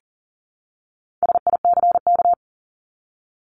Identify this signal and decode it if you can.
Morse code, heard as a keyed tone, about 40 wpm, 700 Hz. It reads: SICK
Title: Top 1501-1750 Words 40wpm